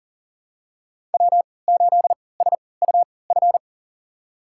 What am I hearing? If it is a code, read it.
Morse code, heard as a keyed tone, about 40 wpm, spaced as for 23 wpm, 700 Hz. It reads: W8SUF
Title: USA Call Signs - 5 Characters - 3x Character Spacing - 40wpm